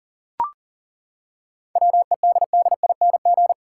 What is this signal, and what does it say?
Morse code, heard as a keyed tone, about 40 wpm, 700 Hz. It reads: WEDDING
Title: Seven Letter Words - 40wpm